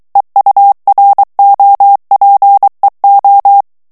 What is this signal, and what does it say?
Morse code, heard as a keyed tone, about 23 wpm, 800 Hz. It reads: EUROPEO
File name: morse mp3 / ESPAGNOL TEXTE EN CLAIR 2 VITESSE 22 MOTS MINUTES